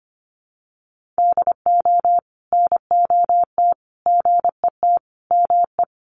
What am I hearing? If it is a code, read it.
Morse code, heard as a keyed tone, about 25 wpm, 700 Hz. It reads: DO NOT GET ME
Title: Sets of 5 Words Encore 25wpm